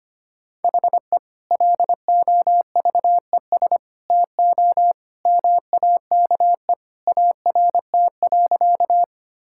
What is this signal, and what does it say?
Morse code, heard as a keyed tone, about 25 wpm, 700 Hz. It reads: HE LOVES TO MAKE ART.
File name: practice / 25wpm / Sentences with 2-Word Phrases - 25wpm